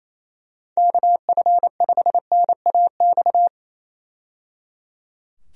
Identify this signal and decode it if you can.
Morse code, heard as a keyed tone, about 28 wpm, 700 Hz. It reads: KF5NAX
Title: USA Call Signs - 6 Characters - 28wpm